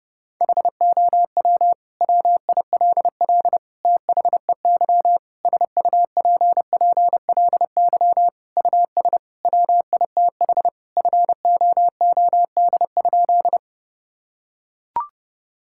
Morse code, heard as a keyed tone, about 32 wpm, 700 Hz. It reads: HOW WILL THEY SUPPLY US WITH FOOD?
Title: Sentences from Top 1000 Words - Review Part 5 - 30wpm